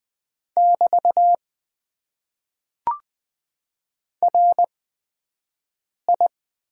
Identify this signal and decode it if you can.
Morse code, heard as a keyed tone, about 20 wpm, spaced as for 5 wpm, 700 Hz. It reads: <BT> RI
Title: Top 90 QSO Elements - 8x Character Spacing - 20wpm